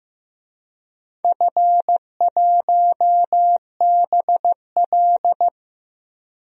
Morse code, heard as a keyed tone, about 15 wpm, 700 Hz. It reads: F1BL
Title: DXCC-100 Call Signs - 4 Characters - 15wpm